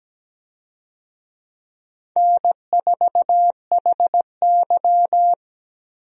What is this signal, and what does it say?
Morse code, heard as a keyed tone, about 17 wpm, 700 Hz. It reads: N4HY
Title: USA Call Signs - 4 Characters - Repeated 2x - 17wpm